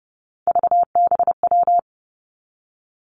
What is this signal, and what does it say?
Morse code, heard as a keyed tone, about 30 wpm, 700 Hz. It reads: VBW
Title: Three Challenging Letter-Number Combinations 30wpm